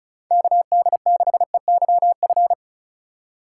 Morse code, heard as a keyed tone, about 36 wpm, 700 Hz. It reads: KD6EYF
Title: USA Call Signs - 6 Characters - 3x Character Spacing - 35wpm